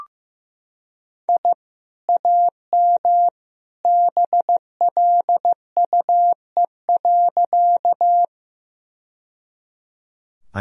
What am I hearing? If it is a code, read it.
Morse code, heard as a keyed tone, about 15 wpm, 700 Hz. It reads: I AM BLUE.
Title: Sentences from Top 1000 Words - Review Part 3 - 15wpm